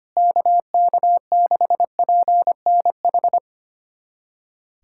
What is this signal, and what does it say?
Morse code, heard as a keyed tone, about 25 wpm, 700 Hz. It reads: KK6PNH
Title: USA Callsign DE Callsign - 6 Characters - 25wpm